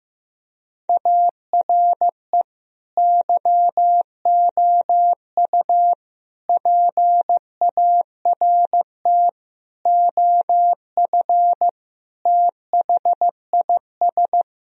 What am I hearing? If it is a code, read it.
Morse code, heard as a keyed tone, about 15 wpm, 700 Hz. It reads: ARE YOU PART OF THIS
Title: Sentences from Top 1000 Words - Review Part 2 - 15wpm